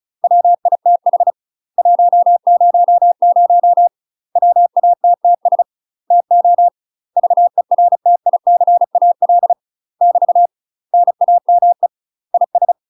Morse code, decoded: WITH 100 WATTS TO VERTICAL <BT> NAME IS

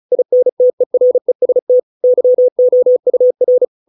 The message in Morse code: INTEREST YOUR

